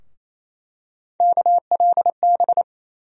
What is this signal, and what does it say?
Morse code, heard as a keyed tone, about 28 wpm, 700 Hz. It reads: KLB
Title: USA Call Sign Postfixes - 3 Letters 28wpm